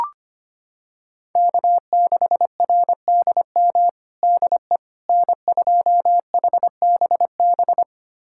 Morse code, heard as a keyed tone, about 25 wpm, 700 Hz. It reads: K6RDM DE N2HBB
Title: USA Callsign DE Callsign - 5 Characters - 25wpm